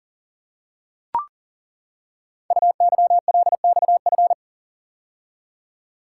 Morse code, heard as a keyed tone, about 40 wpm, 700 Hz. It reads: UYLXF
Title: Code Groups - 5 Letters - 40wpm